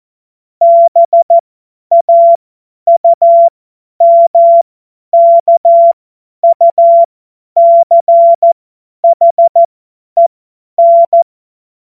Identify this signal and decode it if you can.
Morse code, heard as a keyed tone, about 14 wpm, spaced as for 10 wpm, 680 Hz. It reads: BAUMKUCHEN
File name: Morsekurs / Witze / Witz 18. Mai 2021, 50 F70